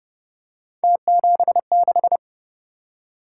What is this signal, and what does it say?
Morse code, heard as a keyed tone, about 28 wpm, 700 Hz. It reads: T76